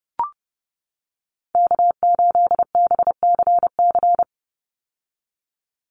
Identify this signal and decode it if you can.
Morse code, heard as a keyed tone, about 30 wpm, 700 Hz. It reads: K8BCC
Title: USA Call Signs 30wpm